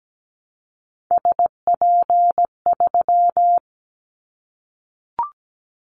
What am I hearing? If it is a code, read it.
Morse code, heard as a keyed tone, about 17 wpm, 700 Hz. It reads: SP3
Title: Callsign Prefixes for Top 25 DXCC - Repeated 3x - 17wpm